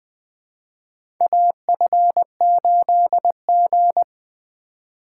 Morse code, heard as a keyed tone, about 20 wpm, 700 Hz. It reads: AF8G
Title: USA Call Signs - 3x Character Spacing - 20wpm